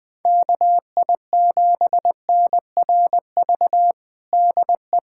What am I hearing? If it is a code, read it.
Morse code, heard as a keyed tone, about 21 wpm, 700 Hz. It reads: KI7NRV DE
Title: USA Callsign DE Callsign 20wpm